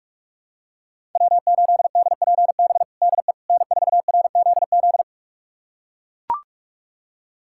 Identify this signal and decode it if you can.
Morse code, heard as a keyed tone, about 42 wpm, 700 Hz. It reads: W8DPB DE N4RZZ
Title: USA Callsign DE Callsign - 5 Characters - 45wpm